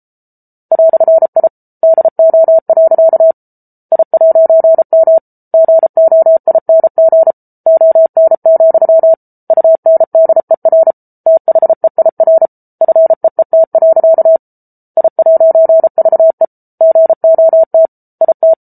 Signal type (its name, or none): Morse code